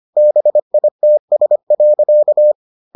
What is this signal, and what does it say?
Morse code, heard as a keyed tone, about 25 wpm, 600 Hz. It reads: BITS.